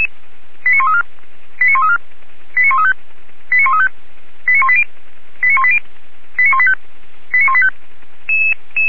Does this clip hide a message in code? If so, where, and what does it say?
Yes: five-tone selective call (ZVEI) from 0.7 to 7.7 s, heard as quick runs of five tones, tones of about 70 ms. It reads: 89134 89134 89135 89135 89180 89180 89185 89185